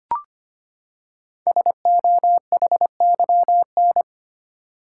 Morse code, heard as a keyed tone, about 25 wpm, 700 Hz. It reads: SOHYN
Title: Code Groups - 5 Letters - 25wpm